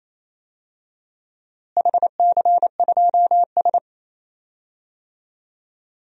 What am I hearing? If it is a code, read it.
Morse code, heard as a keyed tone, about 28 wpm, 700 Hz. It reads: HC2S
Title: DXCC-100 Call Signs - 4 Characters - Repeated 2x - 28wpm